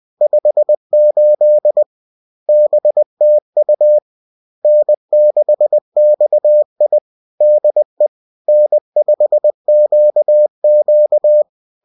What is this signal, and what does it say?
Morse code, heard as a keyed tone, about 20 wpm, 600 Hz. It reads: 58 BTU N6XI DE N5QQ